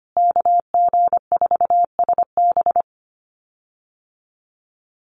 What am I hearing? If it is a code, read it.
Morse code, heard as a keyed tone, about 25 wpm, 700 Hz. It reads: KG4SB
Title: USA Call Signs Repeated 3x 25wpm